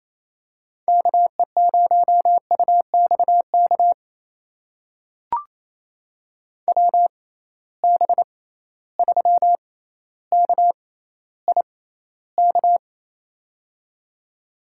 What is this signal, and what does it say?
Morse code, heard as a keyed tone, about 28 wpm, spaced as for 14 wpm, 700 Hz. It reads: KE0UXK WB3KIK
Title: USA Call Signs - 6 Characters - 6x Character Spacing - 28wpm